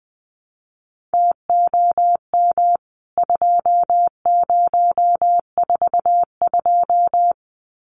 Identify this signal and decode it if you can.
Morse code, heard as a keyed tone, about 20 wpm, 700 Hz. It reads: TOM 2042
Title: CWT Exchange 20wpm